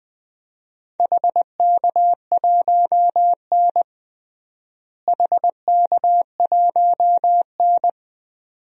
Morse code, heard as a keyed tone, about 20 wpm, 700 Hz. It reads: HK1N HK1N
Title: DXCC-100 Call Signs - 4 Characters - Repeated 2x - 20wpm